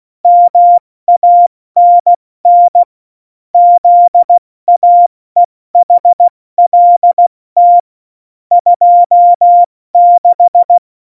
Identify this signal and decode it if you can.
Morse code, heard as a keyed tone, about 16 wpm, 700 Hz. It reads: MANN ZAEHLT 26
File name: Morsekurs / Witze / Witz 15. Oktober 2019, 70 F80